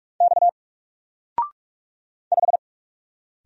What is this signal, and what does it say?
Morse code, heard as a keyed tone, about 46 wpm, 700 Hz. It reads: X 5
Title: Single Character (Letter:Number:Punctuation:Prosign) 45wpm